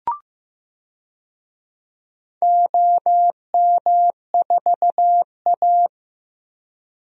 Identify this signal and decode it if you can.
Morse code, heard as a keyed tone, about 15 wpm, 700 Hz. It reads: OM4A
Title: DXCC-100 Call Signs - 4 Characters - Repeated 2x - 15wpm